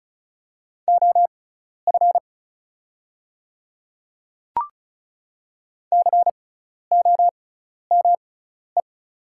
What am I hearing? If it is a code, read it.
Morse code, heard as a keyed tone, about 36 wpm, spaced as for 12 wpm, 700 Hz. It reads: OF COME
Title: 6x Character Spacing - Top 100 Words - 35wpm